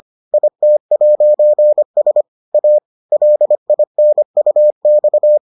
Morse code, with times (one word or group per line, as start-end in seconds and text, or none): 0.3-2.2 IT'S
2.5-2.8 A
3.1-5.4 LINUX